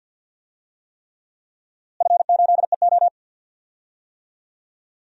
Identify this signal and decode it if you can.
Morse code, heard as a keyed tone, about 52 wpm, 700 Hz. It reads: F8EO
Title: DXCC-100 Call Signs - 4 Characters - Repeated 3x - 50wpm